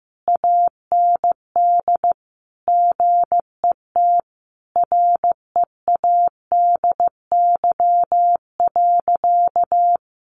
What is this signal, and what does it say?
Morse code, heard as a keyed tone, about 14 wpm, 700 Hz. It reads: AND GET READY.
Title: Sentences from Top 700 Words 15wpm